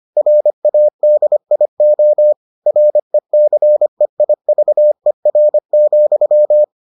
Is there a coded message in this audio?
Yes: Morse code from 0.2 to 6.6 s, heard as a keyed tone, about 25 wpm, 600 Hz. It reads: RADIO RECEIVER,